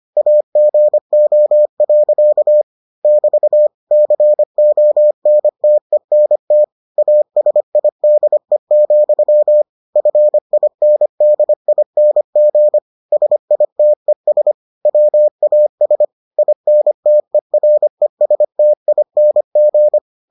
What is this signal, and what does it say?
Morse code, heard as a keyed tone, about 25 wpm, 600 Hz. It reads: AGO. <BT> CONTENT ASIDE, FINDING SITES WAS INTERESTING